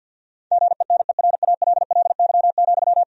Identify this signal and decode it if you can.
Morse code, heard as a keyed tone, about 50 wpm, 700 Hz. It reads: GENERALLY,
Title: Sentences from Top 1000 Words - Review Part 4 - 50wpm